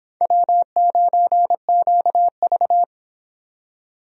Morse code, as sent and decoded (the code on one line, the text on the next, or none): .-- ----. --.- ...-
W9QV